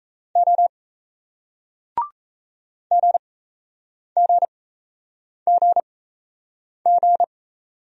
Morse code, heard as a keyed tone, about 37 wpm, spaced as for 8 wpm, 700 Hz. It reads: O GGGG